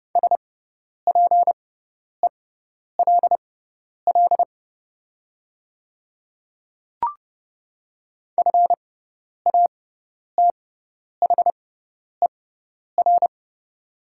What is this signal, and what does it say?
Morse code, heard as a keyed tone, about 30 wpm, spaced as for 10 wpm, 700 Hz. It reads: SPELL FATHER
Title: 6x Character Spacing - Top 200 Words - 30wpm